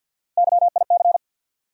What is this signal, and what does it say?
Morse code, heard as a keyed tone, about 52 wpm, 690 Hz. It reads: YIC
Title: USA Call Sign Postfixes - 3 Letters 50wpm